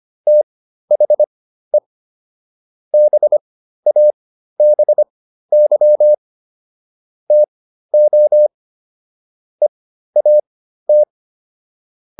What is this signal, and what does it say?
Morse code, heard as a keyed tone, about 25 wpm, spaced as for 13 wpm, 600 Hz. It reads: THE BABY TO EAT